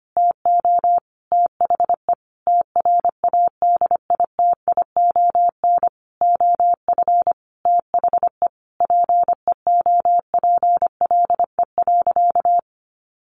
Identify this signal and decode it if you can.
Morse code, heard as a keyed tone, about 25 wpm, 700 Hz. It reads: TO THE TRADITION OF THE PEOPLE.
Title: Sentences from Top 1500 Words 25wpm